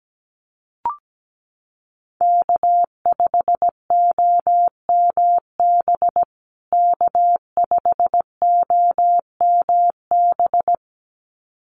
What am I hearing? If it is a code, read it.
Morse code, heard as a keyed tone, about 17 wpm, 700 Hz. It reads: K5OMB K5OMB
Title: USA Call Signs Repeated 2x 17wpm